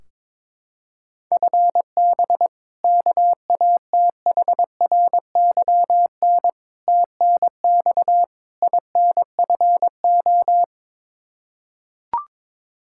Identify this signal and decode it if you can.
Morse code, heard as a keyed tone, about 21 wpm, 700 Hz. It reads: FB KATHRYN TNX INFO
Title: Practice QSO - Exchange 3 - More Info - 22wpm